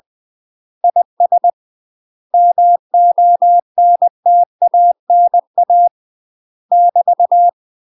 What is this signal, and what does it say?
Morse code, heard as a keyed tone, about 20 wpm, 700 Hz. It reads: IS MONTANA <BT>
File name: UK MASTER QSOs TRAINING/20WPM